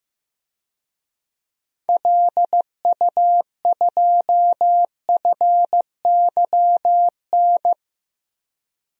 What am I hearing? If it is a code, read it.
Morse code, heard as a keyed tone, about 15 wpm, 700 Hz. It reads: LU2FYN
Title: DXCC-100 Call Signs - 6 Characters - 15wpm